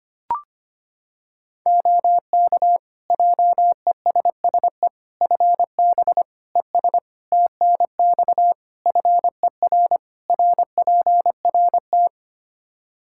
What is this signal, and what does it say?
Morse code, heard as a keyed tone, about 25 wpm, 700 Hz. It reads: OK JESSE FB ES TNX FER RPRT
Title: Practice QSO - Exchange 2 - Info - 25wpm